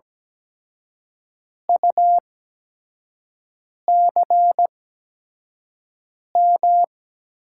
Morse code, heard as a keyed tone, about 17 wpm, spaced as for 5 wpm, 700 Hz. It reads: UCM
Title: USA Call Signs - 6 Characters - 8x Character Spacing - 17wpm